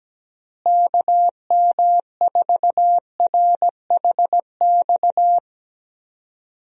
Morse code, heard as a keyed tone, about 17 wpm, 700 Hz. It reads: KM4RHX
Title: USA Call Signs - 6 Characters - Repeated 3x - 17wpm